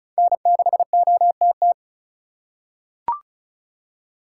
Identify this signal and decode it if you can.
Morse code, heard as a keyed tone, about 36 wpm, 700 Hz. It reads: N6OTT
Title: USA Call Signs - 5 Characters - 8x Character Spacing - 35wpm